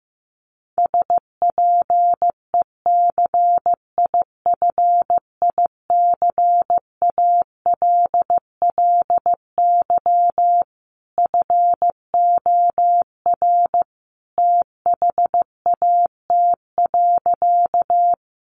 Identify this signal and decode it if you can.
Morse code, heard as a keyed tone, about 15 wpm, 700 Hz. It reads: SPECIFICALLY FOR THAT.
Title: Sentences from Top 1100 Words 15wpm